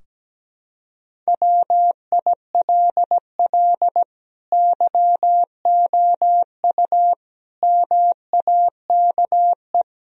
Morse code, heard as a keyed tone, about 17 wpm, 700 Hz. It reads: WILL YOU MAKE
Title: Sentences from Top 1000 Words - Review Part 3 - 17wpm